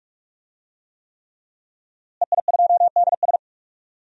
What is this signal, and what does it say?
Morse code, heard as a keyed tone, about 46 wpm, 700 Hz. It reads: EI2DS
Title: DXCC-100 Call Signs - 5 Characters - Repeated 3x - 45wpm